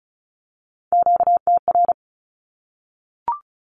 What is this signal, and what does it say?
Morse code, heard as a keyed tone, about 34 wpm, 700 Hz. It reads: QTR
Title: Q-Codes 35wpm